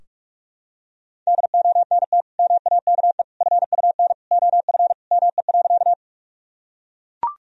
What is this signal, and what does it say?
Morse code, heard as a keyed tone, about 41 wpm, 700 Hz. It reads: DONT MAKE FUN OF ME.
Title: Sentences from Top 1000 Words - Review Part 4 - 45wpm